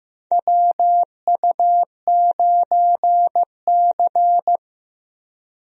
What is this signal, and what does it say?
Morse code, heard as a keyed tone, about 15 wpm, 700 Hz. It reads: WU9C
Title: USA Call Signs - 4 Characters - Repeated 2x - 15wpm